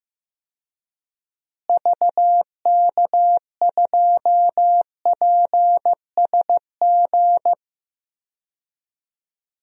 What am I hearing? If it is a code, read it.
Morse code, heard as a keyed tone, about 15 wpm, 700 Hz. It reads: VK2PSG
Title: DXCC-100 Call Signs - 6 Characters - Repeated 2x - 15wpm